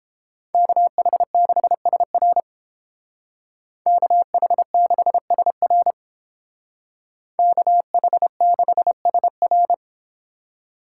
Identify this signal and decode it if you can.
Morse code, heard as a keyed tone, about 28 wpm, 700 Hz. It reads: KH6SR KH6SR KH6SR